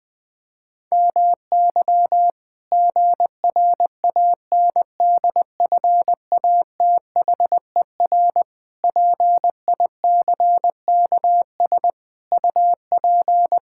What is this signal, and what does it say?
Morse code, heard as a keyed tone, about 20 wpm, 700 Hz. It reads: MY GRANDFATHER PICKS UP
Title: Pangram Challenge v2 - 20wpm